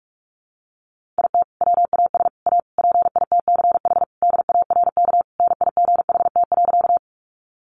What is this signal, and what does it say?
Morse code, heard as a keyed tone, about 44 wpm, 700 Hz. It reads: IT WAS A PITCH DARK NIGHT.
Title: Sentences from Top 1300 Words 45wpm